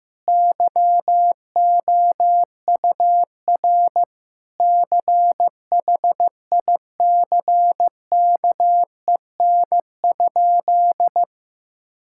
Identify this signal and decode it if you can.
Morse code, heard as a keyed tone, about 15 wpm, 700 Hz. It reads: YOUR CHICKEN?
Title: Sentences from Top 1000 Words - Review Part 5 - 15wpm